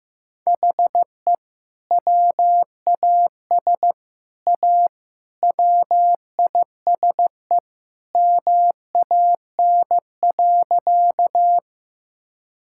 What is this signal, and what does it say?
Morse code, heard as a keyed tone, about 15 wpm, 700 Hz. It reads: HE WAS A WISE MAN.
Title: Sentences from Top 1600 Words 15wpm